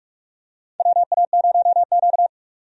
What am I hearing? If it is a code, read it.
Morse code, heard as a keyed tone, about 46 wpm, 700 Hz. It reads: WA0Q
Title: USA Call Signs - 4 Characters - 45wpm